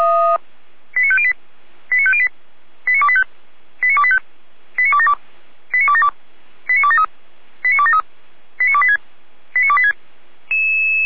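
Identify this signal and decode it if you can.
Five-tone selective call (ZVEI), heard as quick runs of five tones, tones of about 70 ms. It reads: 89508 89508 89285 89285 89272 89272 89273 89273 89286 89286
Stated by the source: ZVEI five-tone call received by SDR.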